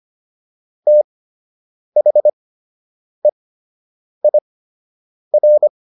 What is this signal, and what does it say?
Morse code, heard as a keyed tone, about 25 wpm, spaced as for 8 wpm, 600 Hz. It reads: THEIR